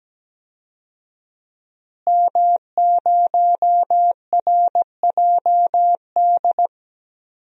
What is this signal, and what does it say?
Morse code, heard as a keyed tone, about 17 wpm, 700 Hz. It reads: M0RJD
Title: DXCC-100 Call Signs - 5 Characters - Repeated 3x - 17wpm